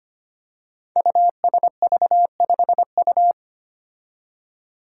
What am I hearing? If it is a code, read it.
Morse code, heard as a keyed tone, about 25 wpm, 700 Hz. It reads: USV5U
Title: Five Challenging Letter-Number Combinations 25wpm